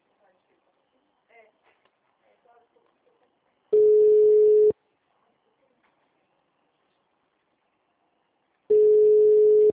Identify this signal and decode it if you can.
Ringback tone (Europe, 425 Hz), with speech.